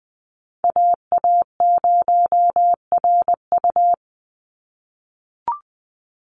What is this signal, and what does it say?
Morse code, heard as a keyed tone, about 20 wpm, 700 Hz. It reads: AA0RU
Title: USA Call Signs 20wpm